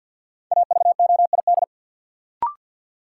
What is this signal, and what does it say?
Morse code, heard as a keyed tone, about 47 wpm, 700 Hz. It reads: AVOID
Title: Sentences from Top 1000 Words - Review Part 6 - New Word Drill - 50wpm